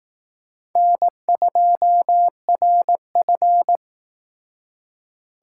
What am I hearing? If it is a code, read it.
Morse code, heard as a keyed tone, about 18 wpm, 700 Hz. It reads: N2RF
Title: Speed Racing - USA 1x2 Call Signs 15wpm